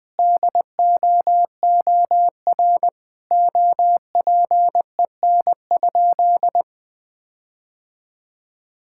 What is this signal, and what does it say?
Morse code, heard as a keyed tone, about 20 wpm, 700 Hz. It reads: DOOR OPEN?
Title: Sentences from Top 1600 Words 20wpm